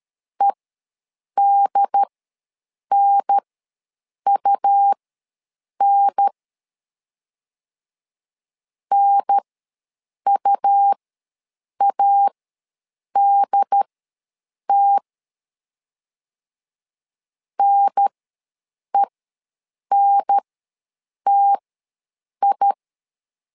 Morse code, heard as a keyed tone, about 13 wpm, spaced as for 7 wpm, 800 Hz. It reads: EDNUN NUADT NENTI
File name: LEZIONI / 2 UHDN / 2 UHDN ws 12wpm 1708